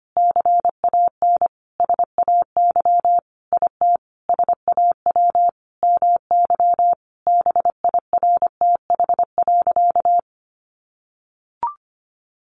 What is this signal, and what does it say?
Morse code, heard as a keyed tone, about 25 wpm, 700 Hz. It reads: CAN SAY IT SAW MY BIRTH.
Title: Sentences from Top 800 Words 25wpm